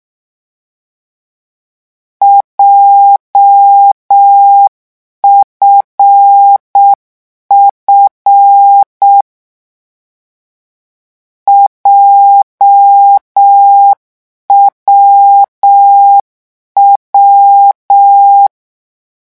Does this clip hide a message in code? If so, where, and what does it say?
Yes: Morse code from 2.2 to 18.5 s, heard as a keyed tone, about 6 wpm, 800 Hz. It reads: JFF JWW